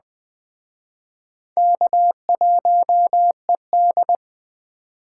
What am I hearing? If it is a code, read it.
Morse code, heard as a keyed tone, about 20 wpm, 700 Hz. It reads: K1ED